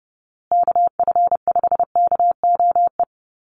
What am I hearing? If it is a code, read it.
Morse code, heard as a keyed tone, about 30 wpm, 700 Hz. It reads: KF5KOE
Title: USA Call Signs Repeated 3x 30wpm